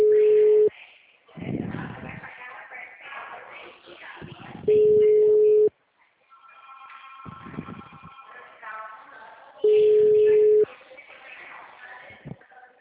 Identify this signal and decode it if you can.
Ringback tone (Europe, 425 Hz), with speech.